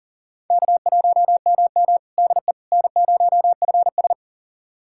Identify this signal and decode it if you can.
Morse code, heard as a keyed tone, about 40 wpm, 700 Hz. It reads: K1MM DE N0FS
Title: USA Callsign DE Callsign - 4 Characters - 40wpm